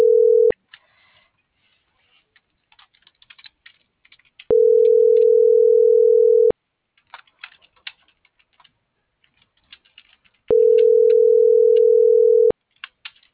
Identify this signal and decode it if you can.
Ringback tone (North America), 440 + 480 Hz.